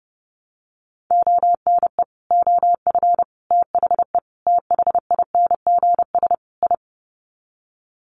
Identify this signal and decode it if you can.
Morse code, heard as a keyed tone, about 30 wpm, 700 Hz. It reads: ONE OF THE THINGS I
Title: Sets of 5 Words 30wpm